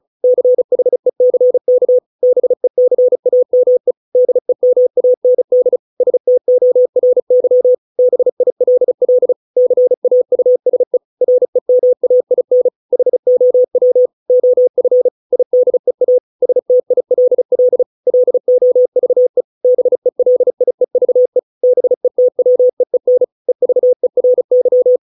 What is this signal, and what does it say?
Morse code, heard as a keyed tone, about 34 wpm, 500 Hz. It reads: CHECK BECAME DEMAND STORY BILL CAUSE REMAIN HOW OF IDEA STILL LOVE BELIEVE BETWEEN EVERY